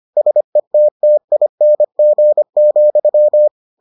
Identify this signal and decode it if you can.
Morse code, heard as a keyed tone, about 25 wpm, 600 Hz. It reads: SETTING,